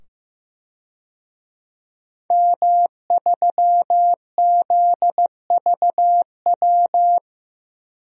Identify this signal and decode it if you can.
Morse code, heard as a keyed tone, about 15 wpm, 700 Hz. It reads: M3ZVW
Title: DXCC-100 Call Signs - 5 Characters - 15wpm